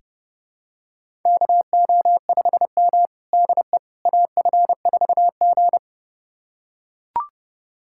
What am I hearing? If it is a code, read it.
Morse code, heard as a keyed tone, about 30 wpm, 700 Hz. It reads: KO5M DE AF4G
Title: USA Callsign DE Callsign - 4 Characters - 30wpm